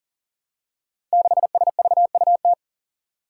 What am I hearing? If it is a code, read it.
Morse code, heard as a keyed tone, about 42 wpm, 700 Hz. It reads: 6SVUT
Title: Five Challenging Letter-Number Combinations 40wpm